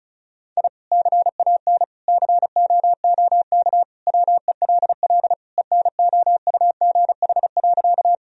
Morse code, heard as a keyed tone, about 36 wpm, 700 Hz. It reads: I CAN COOK WELL ENOUGH.
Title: Sentences from Top 1000 Words - Review Part 4 - 35wpm